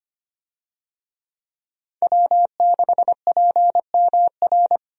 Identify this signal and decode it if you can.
Morse code, heard as a keyed tone, about 25 wpm, 700 Hz. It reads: W6PMR
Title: USA Call Signs - 5 Characters - Repeated 3x - 25wpm